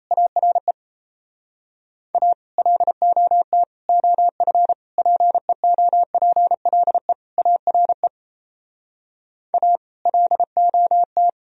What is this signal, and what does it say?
Morse code, heard as a keyed tone, about 35 wpm, 700 Hz. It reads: ARE A LOT OF PEOPLE ARE A LOT
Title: Speed-Racing - Sets of 5 Words 28wpm